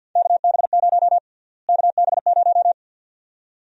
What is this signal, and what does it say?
Morse code, heard as a keyed tone, about 46 wpm, 700 Hz. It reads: KB0 KB0
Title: USA Call Sign Prefixes - Repeated 2x - 50wpm